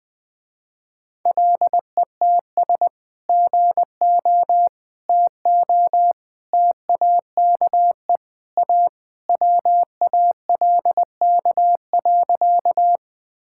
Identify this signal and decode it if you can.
Morse code, heard as a keyed tone, about 20 wpm, 700 Hz. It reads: LETS GO TO TAKE A WALK.
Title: Sentences with 3-Word Phrases - Spoken Beginning - 20wpm